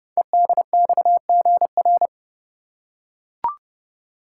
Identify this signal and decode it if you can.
Morse code, heard as a keyed tone, about 28 wpm, 700 Hz. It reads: EDXGR
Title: Code Groups - 5 Letters - 30wpm